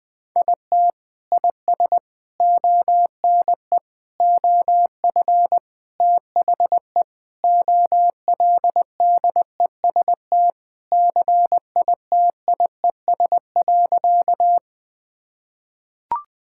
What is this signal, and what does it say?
Morse code, heard as a keyed tone, about 20 wpm, 700 Hz. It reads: IT IS ONE OF THE OLDEST CITIES.
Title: Sentences with 5-Word Phrases - 20wpm